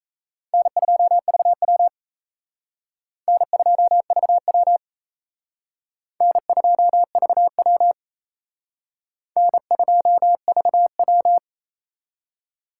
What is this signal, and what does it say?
Morse code, heard as a keyed tone, about 34 wpm, 700 Hz. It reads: N2VW N2VW N2VW N2VW
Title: Speed Racing - USA 1x2 Call Signs 28wpm